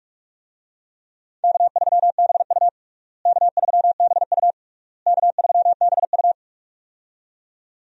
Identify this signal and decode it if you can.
Morse code, heard as a keyed tone, about 46 wpm, 700 Hz. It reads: K3BU K3BU K3BU
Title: USA Call Signs - 4 Characters - Repeated 3x - 45wpm